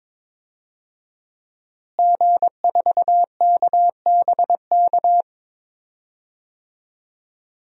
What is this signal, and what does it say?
Morse code, heard as a keyed tone, about 22 wpm, 700 Hz. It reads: G4KBK
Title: DXCC-100 Call Signs - 5 Characters - Repeated 2x - 22wpm